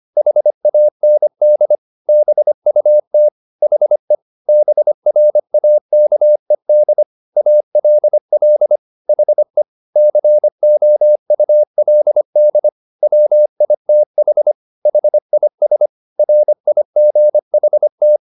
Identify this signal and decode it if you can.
Morse code, heard as a keyed tone, about 25 wpm, 600 Hz. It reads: HAND BUT HE BRAKED ALL HE COULD WITH HIS RIGHT